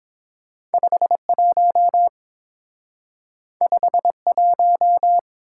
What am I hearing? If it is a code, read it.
Morse code, heard as a keyed tone, about 28 wpm, 700 Hz. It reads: 51 51